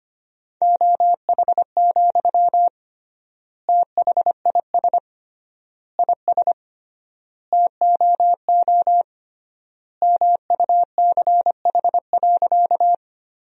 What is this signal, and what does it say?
Morse code, heard as a keyed tone, about 25 wpm, 700 Hz. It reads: OH, THIS IS TOO MUCH.
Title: 3x Word-Spacing - Sentences from Top 500 Words - 25wpm